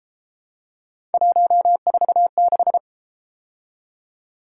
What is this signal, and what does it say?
Morse code, heard as a keyed tone, about 32 wpm, 700 Hz. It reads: 146